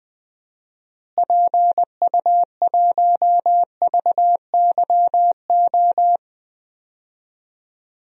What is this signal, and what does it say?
Morse code, heard as a keyed tone, about 19 wpm, 700 Hz. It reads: PU1VYO
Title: DXCC-100 Call Signs - 6 Characters - Repeated 2x - 20wpm